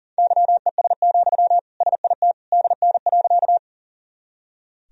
Morse code, heard as a keyed tone, about 40 wpm, 700 Hz. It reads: YES, SIT DN.
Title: Sentences from Top 500 Words and Abbreviations 40wpm